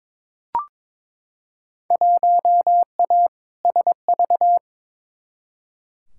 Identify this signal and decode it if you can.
Morse code, heard as a keyed tone, about 22 wpm, 700 Hz. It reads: 1A SV
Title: Field Day Exchanges - 22wpm